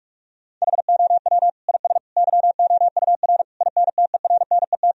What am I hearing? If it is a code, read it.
Morse code, heard as a keyed tone, about 41 wpm, 700 Hz. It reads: HOW IS YOUR INTERNET
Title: Sentences from Top 1000 Words - Review Part 4 - 45wpm